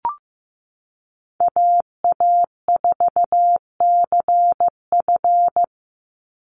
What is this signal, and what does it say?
Morse code, heard as a keyed tone, about 15 wpm, 700 Hz. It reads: AA4CF
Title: Call Signs Repeated 2x - 15wpm